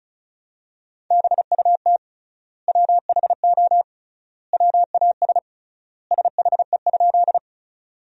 Morse code, heard as a keyed tone, about 36 wpm, 700 Hz. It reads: BUT WHO WAS SHE?
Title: 3x Word-Spacing - Sentences from Top 100 Words - 35wpm